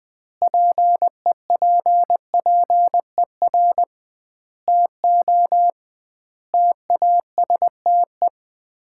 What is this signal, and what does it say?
Morse code, heard as a keyed tone, about 20 wpm, 700 Hz. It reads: PEPPER TO TASTE
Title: 2x Word-Spacing - Sets of 5 Words - 20wpm